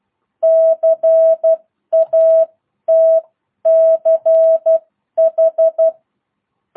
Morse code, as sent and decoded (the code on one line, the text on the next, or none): -.-. .- - -.-. ....
CATCH